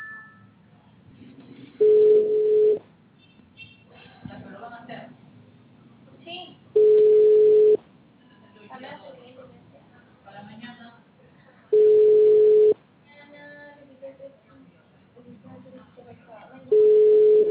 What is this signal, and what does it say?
Ringback tone (Europe, 425 Hz), with speech.